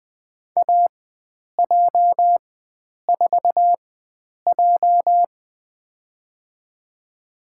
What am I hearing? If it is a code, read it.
Morse code, heard as a keyed tone, about 20 wpm, spaced as for 9 wpm, 700 Hz. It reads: AJ4J
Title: USA Call Signs - 4 Characters - 4x Character Spacing - 20wpm